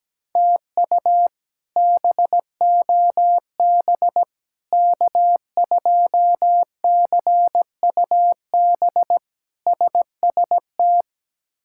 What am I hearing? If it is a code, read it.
Morse code, heard as a keyed tone, about 17 wpm, 700 Hz. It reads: TU BOB K2CUB SST
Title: K1USN SST QSO Practice - 17wpm